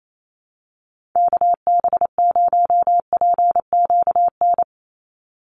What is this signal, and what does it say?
Morse code, heard as a keyed tone, about 28 wpm, 700 Hz. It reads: KB0PQN
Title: USA Call Signs 28wpm